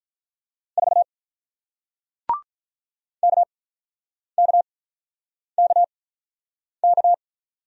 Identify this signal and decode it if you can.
Morse code, heard as a keyed tone, about 48 wpm, spaced as for 9 wpm, 700 Hz. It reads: 4 KKKK